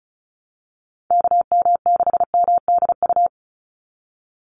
Morse code, heard as a keyed tone, about 34 wpm, 700 Hz. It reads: KM6MDU
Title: USA Call Signs 35wpm